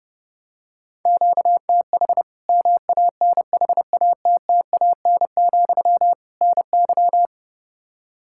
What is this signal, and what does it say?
Morse code, heard as a keyed tone, about 30 wpm, 700 Hz. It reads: QTH MANHATTAN, NY
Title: Practice QSO - Exchange 1 - Beginning - 30wpm